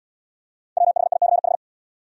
Morse code, heard as a keyed tone, about 79 wpm, 700 Hz. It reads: WHICH